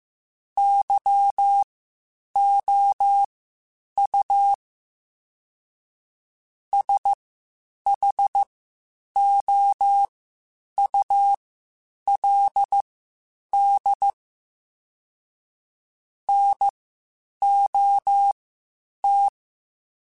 Morse code, heard as a keyed tone, about 15 wpm, spaced as for 8 wpm, 800 Hz. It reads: YOU SHOULD NOT
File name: Spaced / Chuckles Spaced-mp3s / Temptation sp @13